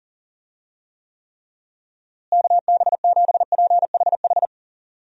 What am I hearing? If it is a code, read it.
Morse code, heard as a keyed tone, about 40 wpm, 700 Hz. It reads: KB7PHH
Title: USA Call Signs - 6 Characters - Repeated 3x - 40wpm